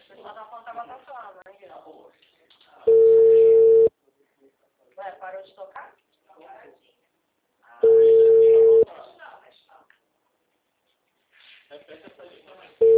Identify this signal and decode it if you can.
Ringback tone (China, 450 Hz), with speech.